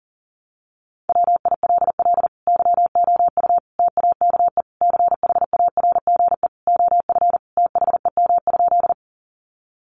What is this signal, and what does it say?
Morse code, heard as a keyed tone, about 40 wpm, 700 Hz. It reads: WILL YOU TAKE CHARGE OF THEM?
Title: Sentences from Top 900 Words 40wpm